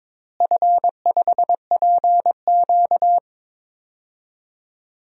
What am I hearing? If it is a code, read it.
Morse code, heard as a keyed tone, about 22 wpm, 700 Hz. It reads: F5PQ
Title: DXCC-100 Call Signs - 4 Characters - Repeated 2x - 22wpm